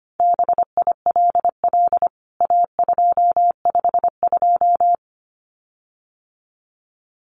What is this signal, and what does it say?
Morse code, heard as a keyed tone, about 25 wpm, 700 Hz. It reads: BILL A252
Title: CWT Exchange 25wpm